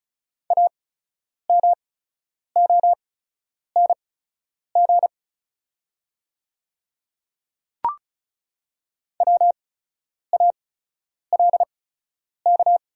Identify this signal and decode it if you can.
Morse code, heard as a keyed tone, about 36 wpm, spaced as for 10 wpm, 690 Hz. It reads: AMONG WALK
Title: 8x Character Spacing - Top 500 Words - 35wpm